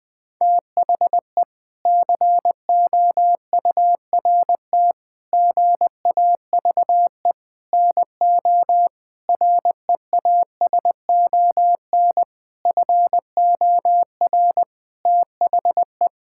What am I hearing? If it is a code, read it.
Morse code, heard as a keyed tone, about 20 wpm, 700 Hz. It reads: THE COURT GAVE NO REASON FOR THE
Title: Sentences from Top 1600 Words 20wpm